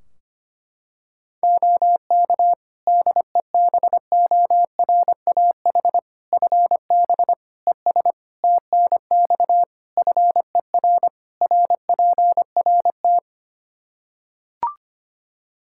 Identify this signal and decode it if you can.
Morse code, heard as a keyed tone, about 25 wpm, 700 Hz. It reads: OK DEBORAH FB ES TNX FER RPRT
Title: Practice QSO - Exchange 2 - Info - 25wpm